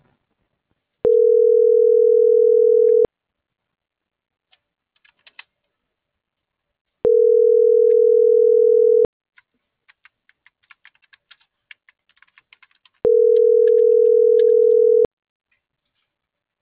Ringback tone (North America), 440 + 480 Hz.
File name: CENTRAL MAYAB / out michelle.martinez - .7058